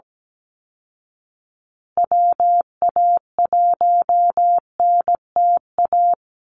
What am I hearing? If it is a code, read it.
Morse code, heard as a keyed tone, about 17 wpm, 700 Hz. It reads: WA1NTA
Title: Call Signs Repeated 2x - 17wpm